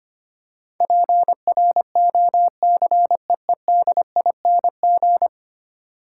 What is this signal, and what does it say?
Morse code, heard as a keyed tone, about 25 wpm, 700 Hz. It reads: PROCEEDING
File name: 10 Letter Words - First 3 Letters Given - 25wpm